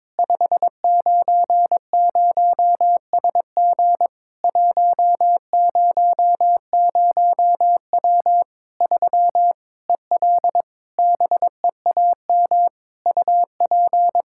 Morse code, heard as a keyed tone, about 21 wpm, 700 Hz. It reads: 590SG 100W 3 EL BEAM UP
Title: K1USN SST QSO Practice 22wpm